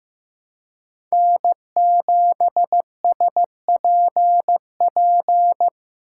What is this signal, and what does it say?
Morse code, heard as a keyed tone, about 15 wpm, 700 Hz. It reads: N7SPP
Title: USA Call Signs - 2x Character Spacing - 15wpm